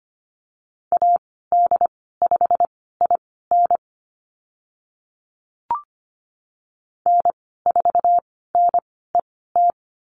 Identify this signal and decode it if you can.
Morse code, heard as a keyed tone, about 25 wpm, spaced as for 16 wpm, 700 Hz. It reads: AD5IN N4NET
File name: Lesson 15g - Number 5 - Call Signs 25wpm